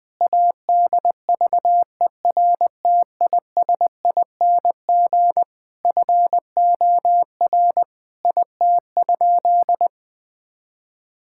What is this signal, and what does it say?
Morse code, heard as a keyed tone, about 20 wpm, 700 Hz. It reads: ADVERTISING FOR IT?
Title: Sentences from Top 1000 Words - Review Part 6 - 20wpm